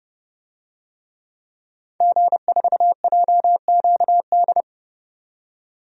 Morse code, heard as a keyed tone, about 30 wpm, 700 Hz. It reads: G4JQD
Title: DXCC-100 Call Signs - 5 Characters - 30wpm